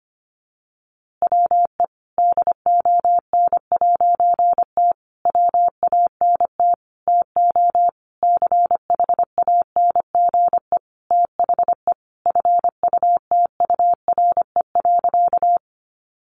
Morse code, heard as a keyed tone, about 25 wpm, 700 Hz. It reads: WE DON'T WANT TO CHANGE THE FUTURE.
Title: A Sound Of Thunder 25wpm